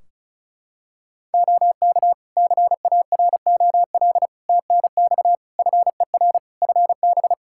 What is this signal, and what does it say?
Morse code, heard as a keyed tone, about 36 wpm, 700 Hz. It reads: OK CAROL TNX FER FB
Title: Practice QSO - Exchange 4 - Ending - 35wpm